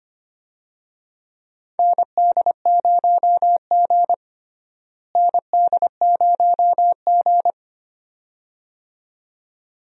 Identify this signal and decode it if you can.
Morse code, heard as a keyed tone, about 25 wpm, 700 Hz. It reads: ND0G ND0G
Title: USA Call Signs - 4 Characters - Repeated 2x - 25wpm